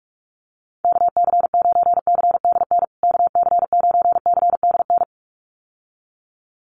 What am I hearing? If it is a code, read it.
Morse code, heard as a keyed tone, about 44 wpm, 700 Hz. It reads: KC9CDN KC9CDN
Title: USA Call Signs Repeated 2x 45wpm